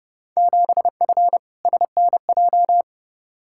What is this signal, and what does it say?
Morse code, heard as a keyed tone, about 32 wpm, 700 Hz. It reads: 7F SNJ
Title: Field Day Exchanges - 30wpm